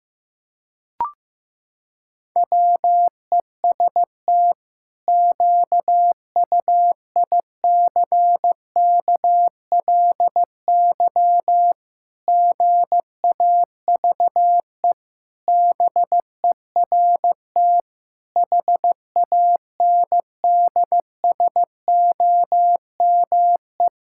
Morse code, heard as a keyed tone, about 15 wpm, 700 Hz. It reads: WEST QUICKLY GAVE BERT HANDSOME
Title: Pangram Challenge v2 - 15wpm